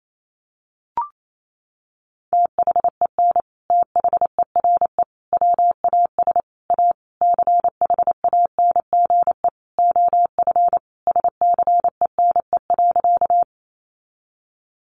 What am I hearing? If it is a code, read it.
Morse code, heard as a keyed tone, about 28 wpm, 700 Hz. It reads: THEN THERE WAS A CHANGE OF SCENE.
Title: Sentences from Top 1100 Words 28wpm